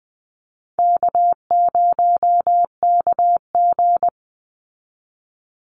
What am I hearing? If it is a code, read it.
Morse code, heard as a keyed tone, about 20 wpm, 700 Hz. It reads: K0KG
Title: Easy Call Signs Repeated 2x - 20wpm